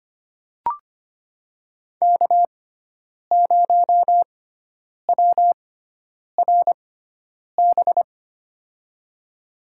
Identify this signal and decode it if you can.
Morse code, heard as a keyed tone, about 25 wpm, spaced as for 9 wpm, 690 Hz. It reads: K0WRB
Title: USA Call Signs - 5 Characters - 6x Character Spacing - 25wpm